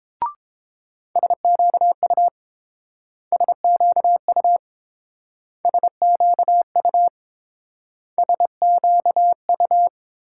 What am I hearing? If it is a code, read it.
Morse code, heard as a keyed tone, about 28 wpm, 700 Hz. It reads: SQU SQU SQU SQU